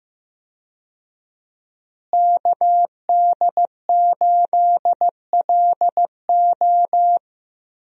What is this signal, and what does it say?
Morse code, heard as a keyed tone, about 15 wpm, 700 Hz. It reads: KD8LO
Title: USA Call Signs - 5 Characters - Repeated 3x - 15wpm